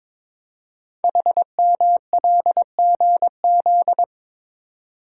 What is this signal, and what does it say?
Morse code, heard as a keyed tone, about 23 wpm, 700 Hz. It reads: HMLGZ
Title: Code Groups - 5 Letters - 22wpm